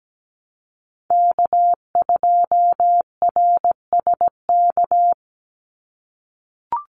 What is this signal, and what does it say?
Morse code, heard as a keyed tone, about 17 wpm, 700 Hz. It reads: K2RSK
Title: Call Signs - 17wpm